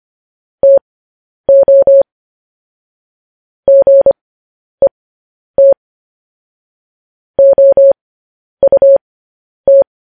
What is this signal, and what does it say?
Morse code, heard as a keyed tone, about 25 wpm, spaced as for 10 wpm, 550 Hz. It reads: TO GET OUT
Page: news-headlines-in-morse-code-at-10-wpm